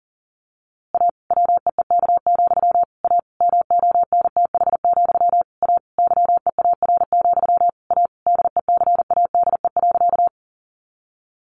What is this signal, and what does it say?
Morse code, heard as a keyed tone, about 40 wpm, 700 Hz. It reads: A WEEK, A MONTH, A YEAR, A DECADE.
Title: A Sound Of Thunder 40wpm